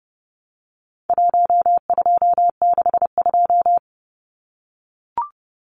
Morse code, heard as a keyed tone, about 30 wpm, 700 Hz. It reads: 1262
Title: Numbers - 4-Digit 30wpm